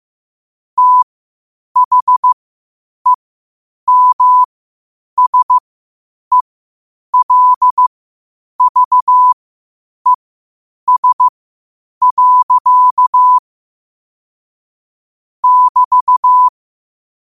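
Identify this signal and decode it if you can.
Morse code, heard as a keyed tone, about 15 wpm, spaced as for 8 wpm, 1000 Hz. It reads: THEMSELVES. <BT>
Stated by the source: Morse code at 8 wpm.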